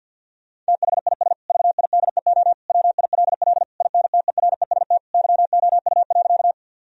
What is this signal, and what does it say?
Morse code, heard as a keyed tone, about 50 wpm, 700 Hz. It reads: THIS VIDEO WILL INTEREST YOU.
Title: Sentences from Top 1000 Words - Review Part 4 - 50wpm